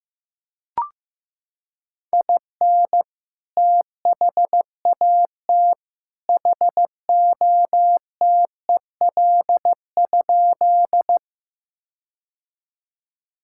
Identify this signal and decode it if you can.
Morse code, heard as a keyed tone, about 15 wpm, 700 Hz. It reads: IN THAT HOTEL?
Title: Sentences from Top 1600 Words 15wpm